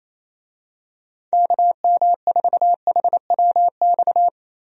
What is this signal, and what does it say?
Morse code, heard as a keyed tone, about 28 wpm, 700 Hz. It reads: KM4HWX
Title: USA Call Signs - 6 Characters - Repeated 2x - 28wpm